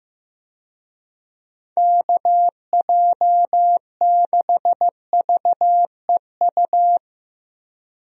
Morse code, heard as a keyed tone, about 15 wpm, 700 Hz. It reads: KJ6VEU
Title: USA Call Signs - 6 Characters - Repeated 3x - 15wpm